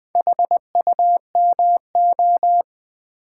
Morse code, heard as a keyed tone, about 20 wpm, 700 Hz. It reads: HUMO
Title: Build A Word - 2 Letters At A Time - 8-Letter Words - 20wpm